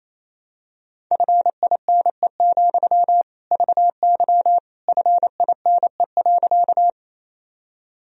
Morse code, heard as a keyed tone, about 28 wpm, 700 Hz. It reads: FINE, VY FINE.
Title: Sentences from Top 500 Words and Abbreviations 28wpm